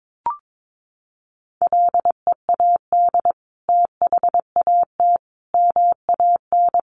Morse code, heard as a keyed tone, about 22 wpm, 700 Hz. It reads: LEAD THAT MAN
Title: Sentences from Top 600 Words 22wpm